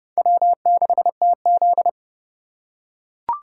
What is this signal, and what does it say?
Morse code, heard as a keyed tone, about 27 wpm, 700 Hz. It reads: W6TZ